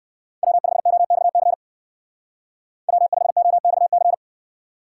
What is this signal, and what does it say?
Morse code, heard as a keyed tone, about 59 wpm, 700 Hz. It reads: W5OXC W5OXC